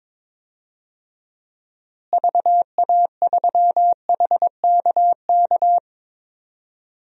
Morse code, heard as a keyed tone, about 22 wpm, 700 Hz. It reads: VA3HKK
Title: DXCC-100 Call Signs - 6 Characters - Repeated 2x - 22wpm